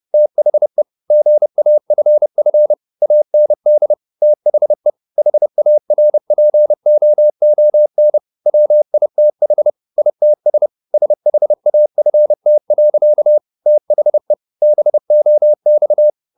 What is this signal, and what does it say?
Morse code, heard as a keyed tone, about 30 wpm, 600 Hz. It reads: THE GAFF AND THE HARPOON WITH ITS SHAFT. THE BOX
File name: the old man and the sea / the old man and the sea 1 30wpm